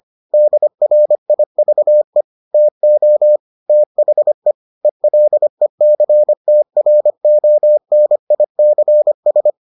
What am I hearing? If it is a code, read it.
Morse code, heard as a keyed tone, about 25 wpm, 600 Hz. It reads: DRIVE TO THE ELECTRONICS